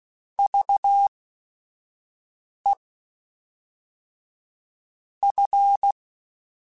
Morse code, heard as a keyed tone, about 16 wpm, spaced as for 5 wpm, 800 Hz. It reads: VE F
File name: Technical Subjects / Technical Subjects-mp3s / XL @5